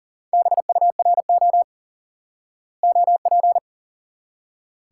Morse code, heard as a keyed tone, about 40 wpm, 700 Hz. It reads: BURO OP
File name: IQE Course 3 - Lesson 89 - YAGI - Two Elements - 40wpm